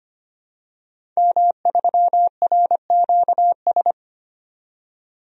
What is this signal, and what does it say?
Morse code, heard as a keyed tone, about 25 wpm, 700 Hz. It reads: M3RQS
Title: DXCC-100 Call Signs - 5 Characters - 25wpm